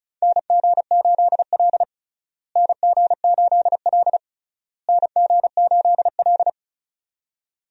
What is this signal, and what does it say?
Morse code, heard as a keyed tone, about 36 wpm, 700 Hz. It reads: NG8L NG8L NG8L
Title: USA Call Signs - 4 Characters - Repeated 3x - 35wpm